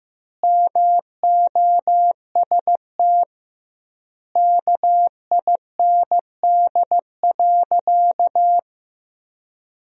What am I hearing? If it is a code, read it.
Morse code, heard as a keyed tone, about 16 wpm, 700 Hz. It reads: MOST KIND.
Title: 2x Word-Spacing - Sentences from Top 200 Words - 15wpm